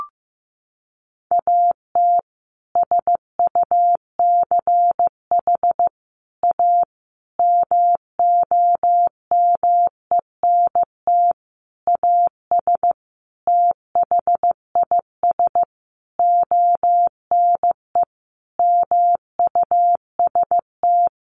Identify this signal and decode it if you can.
Morse code, heard as a keyed tone, about 15 wpm, 700 Hz. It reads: AT SUCH A MOMENT AS THIS ONE MUST
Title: Sentences from Top 1000 Words 15wpm